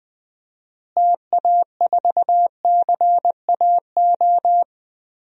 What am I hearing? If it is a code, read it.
Morse code, heard as a keyed tone, about 20 wpm, 700 Hz. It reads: TA4CAO
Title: DXCC-100 Call Signs - 6 Characters - Repeated 2x - 20wpm